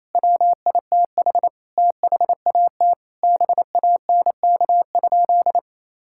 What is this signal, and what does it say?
Morse code, heard as a keyed tone, about 28 wpm, 700 Hz. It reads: WITH THAT BANK?
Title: Sentences from Top 1000 Words - Review Part 6 - 28wpm